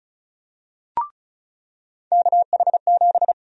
Morse code, heard as a keyed tone, about 36 wpm, 700 Hz. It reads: KH7
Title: USA Call Sign Prefixes - 35wpm